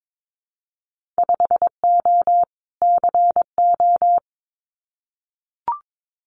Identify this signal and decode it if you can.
Morse code, heard as a keyed tone, about 22 wpm, 700 Hz. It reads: 5O CO
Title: Winter Field Day Exchanges Encore 22wpm